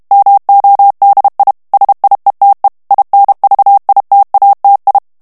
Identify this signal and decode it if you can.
Morse code, heard as a keyed tone, about 32 wpm, 800 Hz. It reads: MODI SIETE INVITATI
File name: ITALIEN texte clair 1 vitesse 30 mots minute